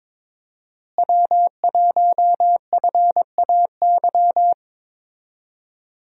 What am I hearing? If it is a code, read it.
Morse code, heard as a keyed tone, about 22 wpm, 700 Hz. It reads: W1FAY
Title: USA Call Signs - 5 Characters - Repeated 2x - 22wpm